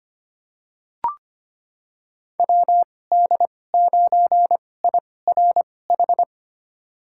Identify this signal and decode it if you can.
Morse code, heard as a keyed tone, about 25 wpm, spaced as for 18 wpm, 700 Hz. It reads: WD9IRH
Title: USA Call Signs - 2x Character Spacing - 25wpm